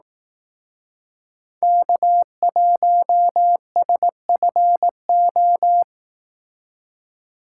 Morse code, heard as a keyed tone, about 19 wpm, 700 Hz. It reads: K1SFO
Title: Speed Racing - USA Call Signs 15wpm